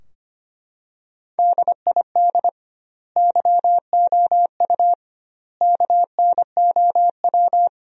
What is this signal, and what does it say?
Morse code, heard as a keyed tone, about 25 wpm, 700 Hz. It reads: DID YOU KNOW
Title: 2x Word-Spacing - Sentences from Top 100 Words - 25wpm